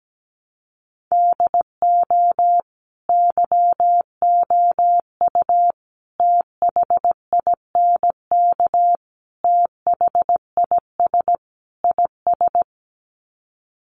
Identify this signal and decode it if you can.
Morse code, heard as a keyed tone, about 17 wpm, 700 Hz. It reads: DO YOU THINK THIS IS
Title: Sets of 5 Words Encore2 17wpm